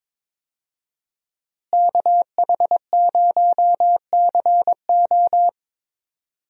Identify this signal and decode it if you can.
Morse code, heard as a keyed tone, about 22 wpm, 700 Hz. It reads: KH0CO
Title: USA Call Signs - 5 Characters - Repeated 2x - 22wpm